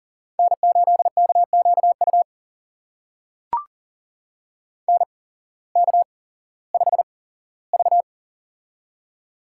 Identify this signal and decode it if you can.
Morse code, heard as a keyed tone, about 42 wpm, spaced as for 19 wpm, 700 Hz. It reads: N8KQU NK5V
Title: USA Call Signs - 8x Character Spacing - 40wpm